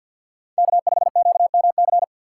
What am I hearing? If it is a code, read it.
Morse code, heard as a keyed tone, about 62 wpm, 700 Hz. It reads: K5QMC
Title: Speed Racing - USA 1x3 Call Signs 50wpm